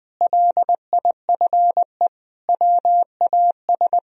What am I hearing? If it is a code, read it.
Morse code, heard as a keyed tone, about 20 wpm, 700 Hz. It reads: LIFE WAS
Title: Sentences from Top 1000 Words - Review Part 2 - 20wpm